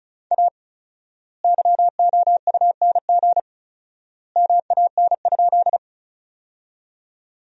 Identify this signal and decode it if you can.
Morse code, heard as a keyed tone, about 34 wpm, 700 Hz. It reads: A YOUNG MAN?
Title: 4x Word-Spacing - Sentences from Top 400 Words - 35wpm